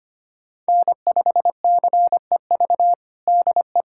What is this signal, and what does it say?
Morse code, heard as a keyed tone, about 25 wpm, 700 Hz. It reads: N5CEV DE
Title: USA Callsign DE Callsign 25wpm